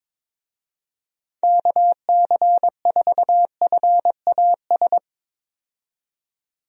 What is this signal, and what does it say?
Morse code, heard as a keyed tone, about 21 wpm, 700 Hz. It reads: KC4FAS
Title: USA Call Signs - 6 Characters - 22wpm